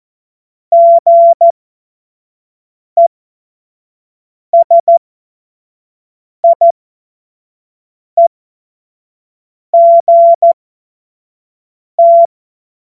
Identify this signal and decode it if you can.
Morse code, heard as a keyed tone, about 14 wpm, spaced as for 5 wpm, 680 Hz. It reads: GESIEGT
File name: Morsekurs / Witze / Witz 21. Juni 2022 25 F70